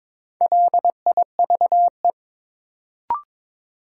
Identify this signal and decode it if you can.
Morse code, heard as a keyed tone, about 22 wpm, 700 Hz. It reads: LIVE
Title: Sentences from Top 200 Words - New Word Drill - 22wpm